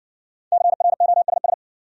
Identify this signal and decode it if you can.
Morse code, heard as a keyed tone, about 60 wpm, 700 Hz. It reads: CROSS